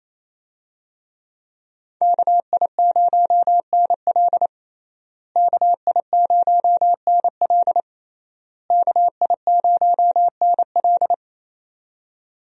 Morse code, heard as a keyed tone, about 27 wpm, 700 Hz. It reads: KI0NL KI0NL KI0NL
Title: USA Call Signs - 5 Characters - Repeated 3x - 28wpm